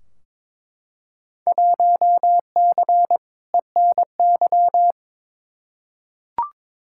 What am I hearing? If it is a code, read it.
Morse code, heard as a keyed tone, about 21 wpm, 700 Hz. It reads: 1C ENY
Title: Field Day Exchanges Encore - 22wpm